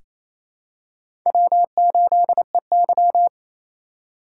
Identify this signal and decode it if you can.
Morse code, heard as a keyed tone, about 27 wpm, 700 Hz. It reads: W8EY
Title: USA Call Signs - 4 Characters - 2x Character Spacing - 28wpm